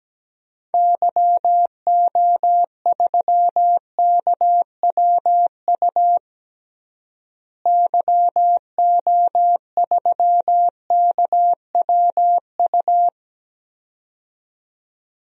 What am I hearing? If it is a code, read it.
Morse code, heard as a keyed tone, about 17 wpm, 700 Hz. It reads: YO3KWU YO3KWU
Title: DXCC-100 Call Signs - 6 Characters - Repeated 2x - 17wpm